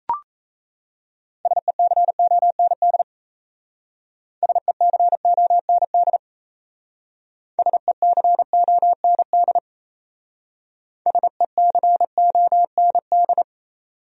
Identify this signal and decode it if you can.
Morse code, heard as a keyed tone, about 33 wpm, 700 Hz. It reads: SECOND SECOND SECOND SECOND
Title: Speed-Racing Top 300 Words 28wpm